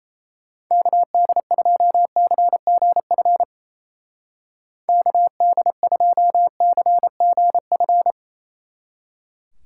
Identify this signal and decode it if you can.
Morse code, heard as a keyed tone, about 37 wpm, 700 Hz. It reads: KD2CGF KD2CGF